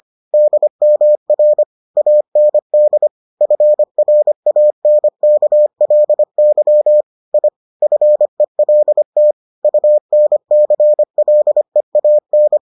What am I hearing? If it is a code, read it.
Morse code, heard as a keyed tone, about 25 wpm, 600 Hz. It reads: DMR AND FRANKLY I FELT UNCLEAN